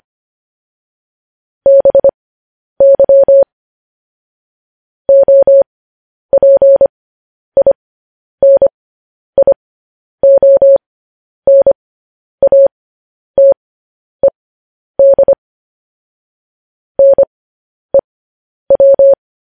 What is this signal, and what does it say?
Morse code, heard as a keyed tone, about 25 wpm, spaced as for 10 wpm, 550 Hz. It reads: BY OPINIONATED NEW